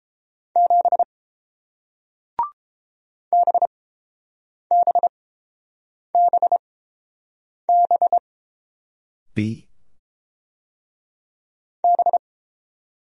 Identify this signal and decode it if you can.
Morse code, heard as a keyed tone, about 30 wpm, spaced as for 5 wpm, 700 Hz. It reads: 7 BBBBB